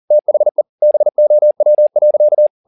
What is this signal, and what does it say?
Morse code, heard as a keyed tone, about 40 wpm, 600 Hz. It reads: THE BOW.
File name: the old man and the sea 2 40wpm